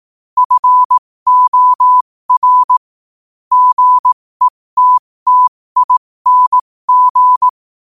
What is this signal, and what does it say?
Morse code, heard as a keyed tone, about 18 wpm, spaced as for 15 wpm, 1000 Hz. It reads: FOR GETTING
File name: qotdfiles / 15wpm